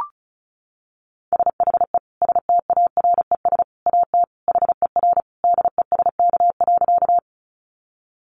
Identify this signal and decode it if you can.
Morse code, heard as a keyed tone, about 34 wpm, 700 Hz. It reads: SHE STARES AT HER DESK.
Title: Easy Phrases 35wpm